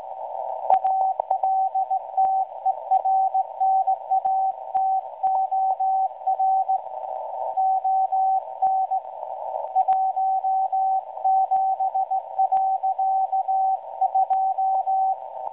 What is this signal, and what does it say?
Morse code, heard as a keyed tone, about 17 wpm, 760 Hz, with radio noise. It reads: ALTERNATOR ON 17.2
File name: WebSDR records / SAQ at PI4THT msg 0800-0804 17.2kHz